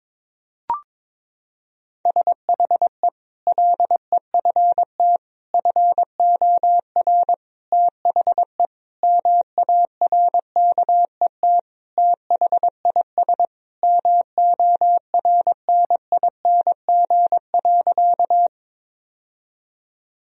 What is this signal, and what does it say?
Morse code, heard as a keyed tone, about 21 wpm, 700 Hz. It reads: SHE LEFT FOR THE MARKET THIS MORNING.
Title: Sentences from Top 1000 Words - Review Part 3 - 22wpm